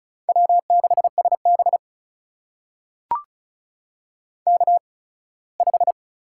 Morse code, heard as a keyed tone, about 36 wpm, spaced as for 21 wpm, 700 Hz. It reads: W6SB K5
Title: USA Call Signs - 4 Characters - 8x Character Spacing - 35wpm